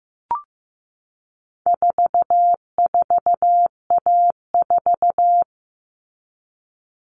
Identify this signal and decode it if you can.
Morse code, heard as a keyed tone, about 15 wpm, 700 Hz. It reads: 44A4
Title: All Cut Numbers - 4-Digit 15wpm